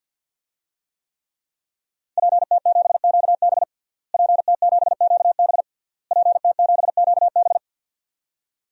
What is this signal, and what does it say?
Morse code, heard as a keyed tone, about 53 wpm, 700 Hz. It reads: PT7QB PT7QB PT7QB
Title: DXCC-100 Call Signs - 5 Characters - Repeated 3x - 50wpm